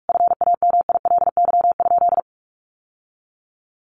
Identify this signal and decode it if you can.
Morse code, heard as a keyed tone, about 44 wpm, 700 Hz. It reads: FAMILY?
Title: Sentences from Top 1100 Words 45wpm